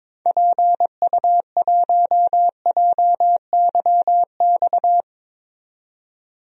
Morse code, heard as a keyed tone, about 22 wpm, 700 Hz. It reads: PU1JYX
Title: DXCC-100 Call Signs - 6 Characters - 22wpm